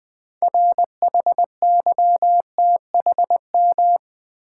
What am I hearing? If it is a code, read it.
Morse code, heard as a keyed tone, about 20 wpm, 700 Hz. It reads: RHYTHM
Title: 6 Letter Words - First 3 Letters Given - 20wpm